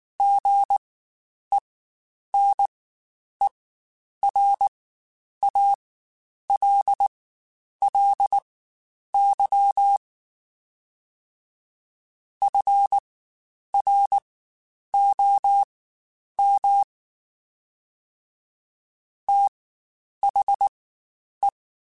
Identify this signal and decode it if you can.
Morse code, heard as a keyed tone, about 19 wpm, spaced as for 9 wpm, 800 Hz. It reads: GENERALLY FROM THE
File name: Spaced Characters / Spaced Characters-mp3s / Weather5 5 sp @18